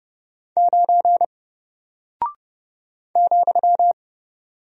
Morse code, heard as a keyed tone, about 30 wpm, 700 Hz. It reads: 9 ,
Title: Single Character (Letter:Number:Punctuation:Prosign) Alt 30wpm